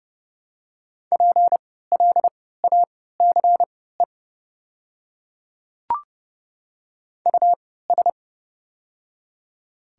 Morse code, heard as a keyed tone, about 30 wpm, spaced as for 17 wpm, 700 Hz. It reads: PLACE US
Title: 3x Character Spacing - Top 200 Words - 30wpm